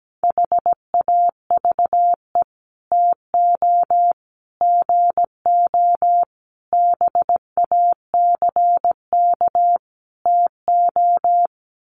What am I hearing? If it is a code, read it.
Morse code, heard as a keyed tone, about 17 wpm, 700 Hz. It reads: HAVE TO GO BACK TO
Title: Sets of 5 Words Encore2 17wpm